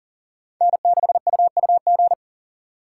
Morse code, heard as a keyed tone, about 52 wpm, 700 Hz. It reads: N6UUG